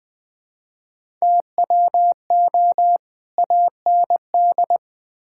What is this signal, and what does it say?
Morse code, heard as a keyed tone, about 20 wpm, 700 Hz. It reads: TWO AND
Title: Build A Phrase - 5-Word Phrases - 20wpm